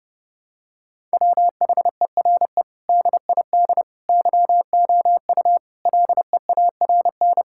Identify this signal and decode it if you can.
Morse code, heard as a keyed tone, about 32 wpm, 700 Hz. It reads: WHERE DID YOU LEARN
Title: Sentences from Top 1000 Words - Review Part 6 - 30wpm